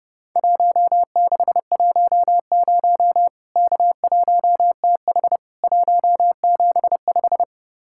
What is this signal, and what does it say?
Morse code, heard as a keyed tone, about 32 wpm, 700 Hz. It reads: 1610 K1TH 175
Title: Worked All Europe DX Contest - QTC Exchange - 30wpm